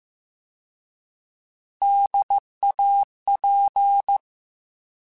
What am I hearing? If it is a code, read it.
Morse code, heard as a keyed tone, about 15 wpm, 800 Hz. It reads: DAP
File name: Alphabetical / Alphabetical mp3s / Ds @13